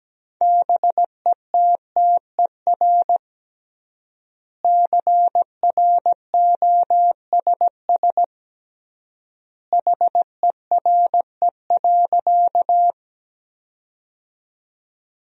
Morse code, heard as a keyed tone, about 17 wpm, 700 Hz. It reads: BETTER CROSS HERE.
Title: 3x Word-Spacing - Sentences from Top 400 Words - 17wpm